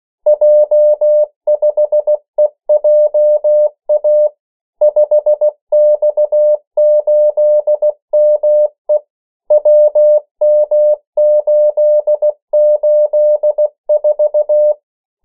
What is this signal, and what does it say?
Morse code, heard as a keyed tone, about 16 wpm, 600 Hz. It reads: J5EJA 5X8ME WM884
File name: morse mp3 / 14 urok 13